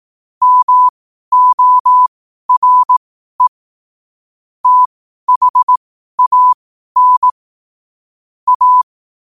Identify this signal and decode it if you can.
Morse code, heard as a keyed tone, about 18 wpm, spaced as for 12 wpm, 1000 Hz. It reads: MORE THAN A
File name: qotdfiles / 12wpm